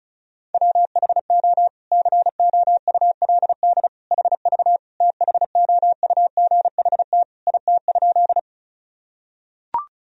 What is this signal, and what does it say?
Morse code, heard as a keyed tone, about 34 wpm, 700 Hz. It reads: WHO COULD HV THOUGHT IT?
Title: Sentences from Top 300 Words and Abbreviations 35wpm